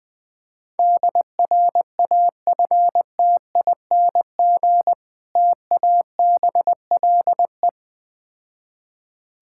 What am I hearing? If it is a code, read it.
Morse code, heard as a keyed tone, about 20 wpm, 690 Hz. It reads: DRAFTING TABLE
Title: Difficult Words with Hints - 20wpm